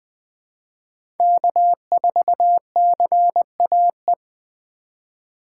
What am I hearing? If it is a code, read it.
Morse code, heard as a keyed tone, about 20 wpm, 700 Hz. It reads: K4CAE
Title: 2023 POTA Hunter Call Signs - 20wpm